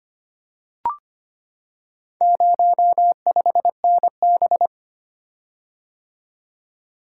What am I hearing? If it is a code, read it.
Morse code, heard as a keyed tone, about 25 wpm, 700 Hz. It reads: 05NB
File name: practice / 25wpm / Four Letter-Number Combinations - 25wpm